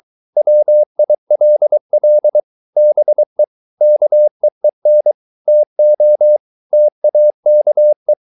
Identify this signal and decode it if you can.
Morse code, heard as a keyed tone, about 23 wpm, 600 Hz. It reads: WILL BE KEEN TO TAKE